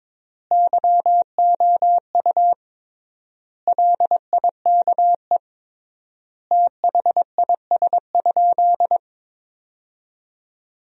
Morse code, heard as a keyed tone, about 22 wpm, 700 Hz. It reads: YOU LIKE THIS?
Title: 3x Word-Spacing - Sentences from Top 200 Words - 22wpm